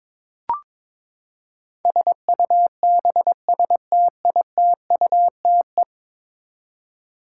Morse code, heard as a keyed tone, about 22 wpm, 700 Hz. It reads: SUBSTITUTE
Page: morse-code-ninja-practice-22wpm